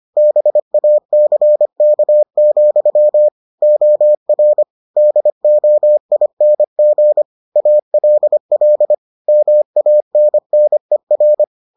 Morse code, heard as a keyed tone, about 25 wpm, 600 Hz. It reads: BACK, OR DOING ALL MANNER